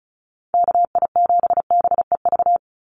Morse code, heard as a keyed tone, about 36 wpm, 700 Hz. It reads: KI7BEV
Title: USA Call Signs 35wpm